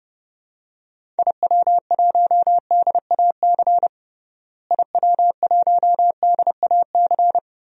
Morse code, heard as a keyed tone, about 30 wpm, 700 Hz. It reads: IW1DAC IW1DAC
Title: DXCC-100 Call Signs - 6 Characters - Repeated 2x - 30wpm